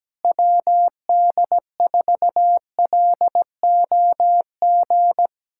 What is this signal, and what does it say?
Morse code, heard as a keyed tone, about 17 wpm, 700 Hz. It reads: WD4LOG
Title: USA Callsign DE Callsign - 6 Characters - 17wpm